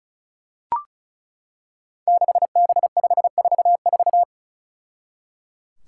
Morse code, heard as a keyed tone, about 36 wpm, 700 Hz. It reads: 6B544
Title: Five Challenging Letter-Number Combinations 35wpm